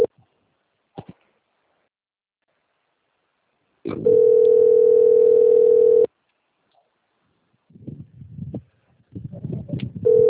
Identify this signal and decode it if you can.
Ringback tone (North America), 440 + 480 Hz.